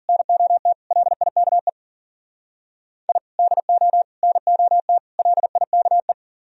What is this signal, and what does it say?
Morse code, heard as a keyed tone, about 51 wpm, 700 Hz. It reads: NOT LIKE I DO NOT LIKE